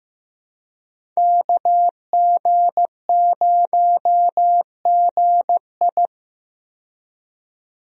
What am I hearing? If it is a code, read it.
Morse code, heard as a keyed tone, about 15 wpm, 700 Hz. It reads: KG0GI
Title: USA Call Signs - 5 Characters - 15wpm